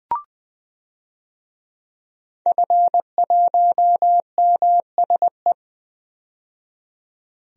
Morse code, heard as a keyed tone, about 20 wpm, 700 Hz. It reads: F1MSE
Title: DXCC-100 Call Signs - 5 Characters - 20wpm